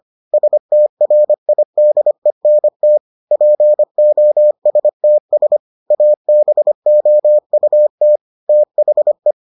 Morse code, heard as a keyed tone, about 25 wpm, 600 Hz. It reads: STRIDENT POSTS ABOUT THE